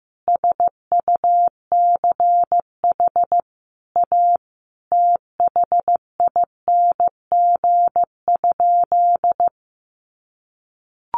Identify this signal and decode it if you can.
Morse code, heard as a keyed tone, about 15 wpm, 700 Hz. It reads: SUCH A THING?
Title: Sentences from Top 1400 Words 15wpm